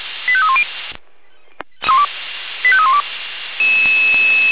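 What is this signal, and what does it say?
Five-tone selective call (ZVEI), heard as quick runs of five tones, tones of about 70 ms. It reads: 85310 85312
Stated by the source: ZVEI five-tone call received by SDR, from the feuerwehr-orlamuende.org fire brigade audio server.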